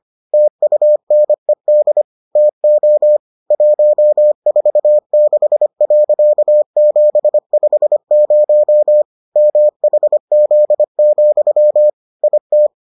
Morse code, heard as a keyed tone, about 25 wpm, 600 Hz. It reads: TUNED TO 146.750 MHZ, IT